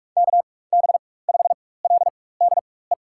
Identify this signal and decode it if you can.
Morse code, heard as a keyed tone, about 42 wpm, spaced as for 21 wpm, 700 Hz. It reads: KB5LDE
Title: USA Call Signs - 4x Character Spacing - 45wpm